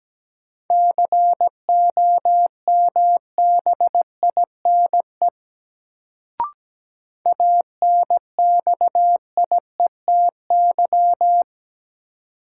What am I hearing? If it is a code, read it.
Morse code, heard as a keyed tone, about 17 wpm, 700 Hz. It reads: COMBINE ANXIETY
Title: Sentences from Top 1600 Words - New Word Drill - 17wpm